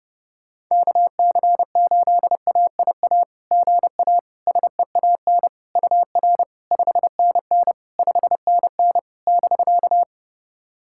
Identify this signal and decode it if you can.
Morse code, heard as a keyed tone, about 30 wpm, 690 Hz. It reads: KC8AIA GA SEAN UR 5NN 5NN <BK>
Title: SOTA QSO Practice 30wpm